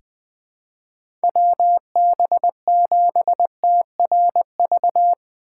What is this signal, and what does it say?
Morse code, heard as a keyed tone, about 20 wpm, 700 Hz. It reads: WB7TRV
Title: USA Call Signs - 6 Characters - 4x Character Spacing - 20wpm